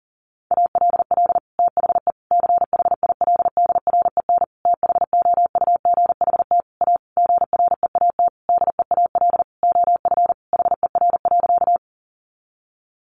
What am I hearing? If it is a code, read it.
Morse code, heard as a keyed tone, about 40 wpm, 700 Hz. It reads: ALL THE CHILDREN THOUGHT A GREAT DEAL OF HER.
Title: Sentences from Top 800 Words 40wpm